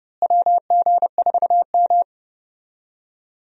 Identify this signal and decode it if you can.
Morse code, heard as a keyed tone, about 30 wpm, 700 Hz. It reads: WG4M
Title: USA Call Signs - 4 Characters - Repeated 3x - 30wpm